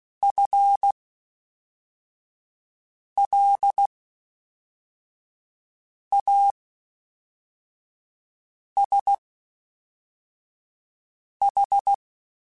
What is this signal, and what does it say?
Morse code, heard as a keyed tone, about 16 wpm, spaced as for 4 wpm, 800 Hz. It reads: FLASH
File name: Spaced / FL Runs sp mp3s / FL Runs sp @5